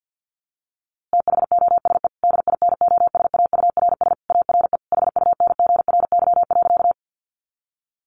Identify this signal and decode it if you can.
Morse code, heard as a keyed tone, about 50 wpm, 700 Hz. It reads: THOSE DINOSAURS ARE HUNGRY.
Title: A Sound Of Thunder 50wpm